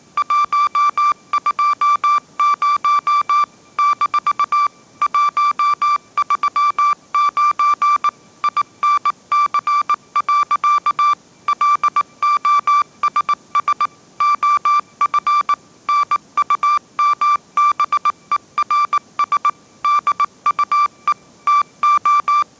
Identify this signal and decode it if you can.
Morse code, heard as a keyed tone, about 19 wpm, 1170 Hz. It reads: 120 - 139 INC. LOSS OF NUMBERS DUE TO
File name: mrx083 20wpm